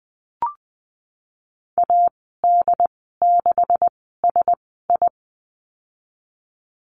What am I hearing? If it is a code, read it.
Morse code, heard as a keyed tone, about 19 wpm, spaced as for 14 wpm, 700 Hz. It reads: AD6SI
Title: 147 Lesson 24g - Letter y - Call Signs 20wpm